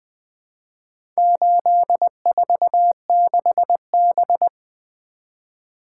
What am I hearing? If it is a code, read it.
Morse code, heard as a keyed tone, about 20 wpm, 700 Hz. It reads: 846B
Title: Four Letter-Number Combinations - 20wpm